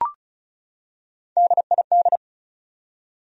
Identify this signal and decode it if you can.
Morse code, heard as a keyed tone, about 36 wpm, 700 Hz. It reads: DID
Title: Top 100 Words 35wpm